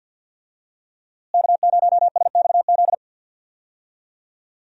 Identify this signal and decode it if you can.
Morse code, heard as a keyed tone, about 50 wpm, 700 Hz. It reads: K0SXZ
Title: USA Call Signs - 5 Characters - Repeated 3x - 50wpm